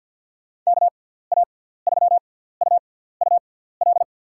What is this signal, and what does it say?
Morse code, heard as a keyed tone, about 52 wpm, spaced as for 17 wpm, 700 Hz. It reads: KA3UUL
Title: USA Call Signs - 6 Characters - 6x Character Spacing - 50wpm